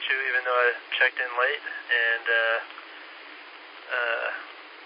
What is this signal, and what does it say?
Emergency Alert System (SAME) tones, heard as harsh buzzing bursts, with speech. It is heard from 0.6 to 4.5 s.